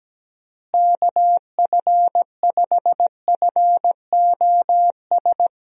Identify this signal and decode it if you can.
Morse code, heard as a keyed tone, about 17 wpm, 700 Hz. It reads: KF5FOS
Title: USA Call Signs - 6 Characters - Repeated 3x - 17wpm